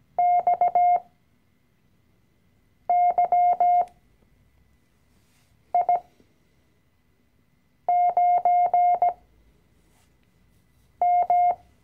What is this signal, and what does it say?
Morse code, heard as a keyed tone, about 17 wpm, spaced as for 4 wpm, 710 Hz. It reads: XYI9M
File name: CW-Online-Kurs / Anfänger / Lektion 012